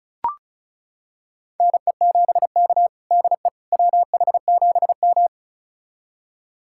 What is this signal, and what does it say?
Morse code, heard as a keyed tone, about 36 wpm, 700 Hz. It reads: NE7K DE WH7M
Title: USA Callsign DE Callsign - 4 Characters - 35wpm